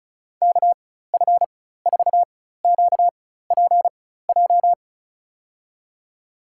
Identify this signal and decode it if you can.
Morse code, heard as a keyed tone, about 34 wpm, spaced as for 16 wpm, 700 Hz. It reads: KF4QPJ
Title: USA Call Signs - 6 Characters - 4x Character Spacing - 35wpm